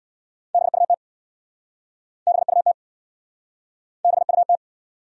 Morse code, heard as a keyed tone, about 70 wpm, 700 Hz. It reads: BUT BUT BUT